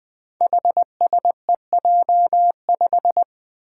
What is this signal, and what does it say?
Morse code, heard as a keyed tone, about 20 wpm, 700 Hz. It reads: HSEJ5
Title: Five Challenging Letter-Number Combinations 20wpm